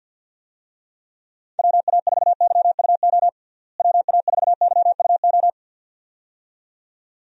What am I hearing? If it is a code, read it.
Morse code, heard as a keyed tone, about 48 wpm, 700 Hz. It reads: WA4YUO WA4YUO
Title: USA Call Signs - 6 Characters - Repeated 2x - 50wpm